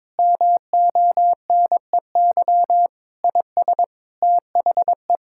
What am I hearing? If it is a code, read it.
Morse code, heard as a keyed tone, about 22 wpm, 700 Hz. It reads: MONEY IS THE
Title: Sentences from Top 1600 Words 22wpm